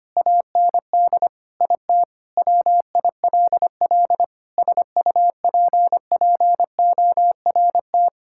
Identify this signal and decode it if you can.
Morse code, heard as a keyed tone, about 25 wpm, 700 Hz. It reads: AND IT WILL SUPPORT
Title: Sentences from Top 1000 Words - Review Part 5 - 25wpm